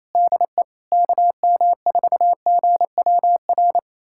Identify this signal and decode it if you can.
Morse code, heard as a keyed tone, about 27 wpm, 700 Hz. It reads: DE KM4GWR
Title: USA Callsign DE Callsign - 6 Characters - 28wpm